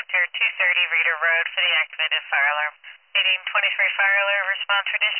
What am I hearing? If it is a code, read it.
Emergency Alert System (SAME) tones, heard as harsh buzzing bursts, with speech. It is heard from 0.9 to 4.1 s.